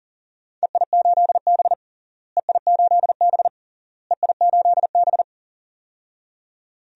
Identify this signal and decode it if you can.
Morse code, heard as a keyed tone, about 40 wpm, 700 Hz. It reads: EI8B EI8B EI8B
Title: DXCC-100 Call Signs - 4 Characters - Repeated 3x - 40wpm